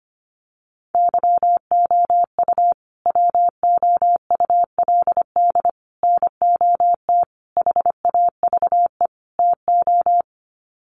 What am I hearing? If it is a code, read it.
Morse code, heard as a keyed tone, about 25 wpm, 700 Hz. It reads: YOU WOULD NOT HAVE TO
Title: Sets of 5 Words Encore 25wpm